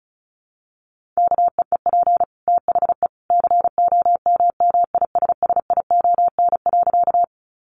Morse code, heard as a keyed tone, about 34 wpm, 700 Hz. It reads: KEEP THE COMMISSION.
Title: Sentences from Top 1400 Words 35wpm